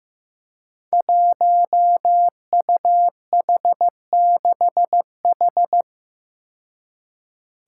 Morse code, heard as a keyed tone, about 15 wpm, 700 Hz. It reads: 1UH6H
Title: Five Challenging Letter-Number Combinations 15wpm